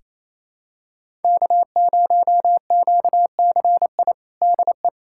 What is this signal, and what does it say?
Morse code, heard as a keyed tone, about 28 wpm, 700 Hz. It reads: K0QCI DE
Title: USA Callsign DE Callsign - 5 Characters - 28wpm